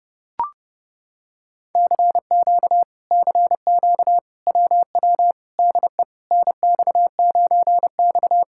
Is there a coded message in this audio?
Yes: Morse code from 1.8 to 8.4 s, heard as a keyed tone, about 30 wpm, 700 Hz. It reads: CQ CQ WW DE NX9X